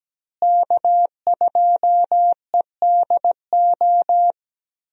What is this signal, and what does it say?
Morse code, heard as a keyed tone, about 17 wpm, 700 Hz. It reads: K2EDO
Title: USA Call Signs - 5 Characters - Repeated 3x - 17wpm